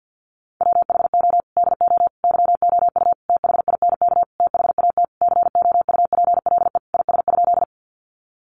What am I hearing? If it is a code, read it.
Morse code, heard as a keyed tone, about 50 wpm, 700 Hz. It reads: WHO DO YOU THINK THAT COUPLE IS?
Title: Sentences from Top 1100 Words 50wpm